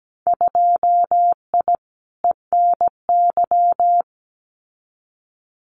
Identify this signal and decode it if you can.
Morse code, heard as a keyed tone, about 17 wpm, 700 Hz. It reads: 2I ENY
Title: Winter Field Day Exchanges Encore 17wpm